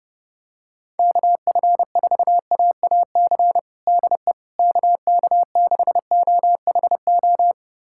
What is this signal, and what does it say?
Morse code, heard as a keyed tone, about 30 wpm, 700 Hz. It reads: KF4AAC DE KK6OHO
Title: USA Callsign DE Callsign 30wpm